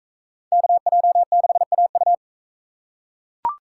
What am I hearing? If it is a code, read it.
Morse code, heard as a keyed tone, about 42 wpm, 700 Hz. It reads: KJ6AU